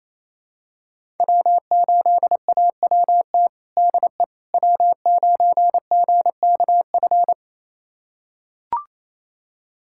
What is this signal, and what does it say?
Morse code, heard as a keyed tone, about 28 wpm, 700 Hz. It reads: W8AWT DE W9GKF
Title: USA Callsign DE Callsign - 5 Characters - 28wpm